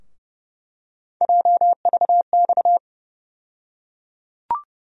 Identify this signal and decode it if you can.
Morse code, heard as a keyed tone, about 30 wpm, 700 Hz. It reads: JVX
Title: USA Call Sign Postfixes - 3 Letters 30wpm